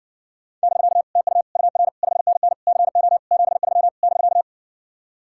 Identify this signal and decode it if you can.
Morse code, heard as a keyed tone, about 69 wpm, 700 Hz. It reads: <BK> TU UR 5NN CO 73 <BK>